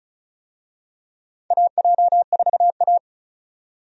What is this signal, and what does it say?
Morse code, heard as a keyed tone, about 34 wpm, 700 Hz. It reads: AJ4A
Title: USA Call Signs - 4 Characters - 35wpm